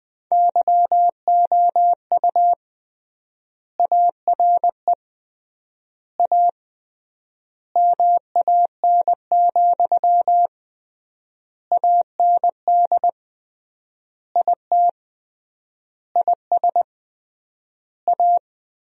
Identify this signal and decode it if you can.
Morse code, heard as a keyed tone, about 20 wpm, 700 Hz. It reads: YOU ARE A MAN, AND IT IS A
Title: 3x Word-Spacing - Sentences from Top 200 Words - 20wpm